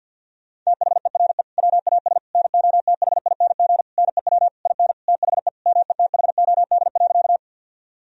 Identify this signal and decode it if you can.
Morse code, heard as a keyed tone, about 51 wpm, 700 Hz. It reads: THERE WAS NOTHING NEW IN THE METHOD.
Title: Sentences from Top 1000 Words - Review Part 3 - 50wpm